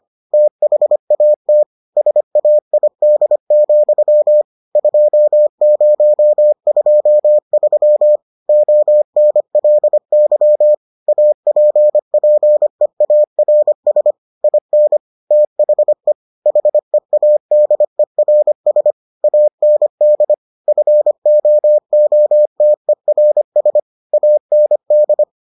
Morse code, heard as a keyed tone, about 25 wpm, 600 Hz. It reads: THAT SAID, 2023 ONLY APPEARS IN THE HEADERS AND FOOTERS AND